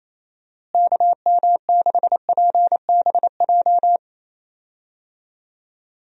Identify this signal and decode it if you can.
Morse code, heard as a keyed tone, about 27 wpm, 700 Hz. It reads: KM6PBJ
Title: USA Call Signs - 6 Characters - Repeated 2x - 28wpm